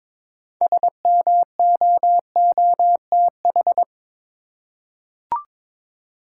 Morse code, heard as a keyed tone, about 22 wpm, 700 Hz. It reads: SMOOTH
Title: Six Letter Words - 22wpm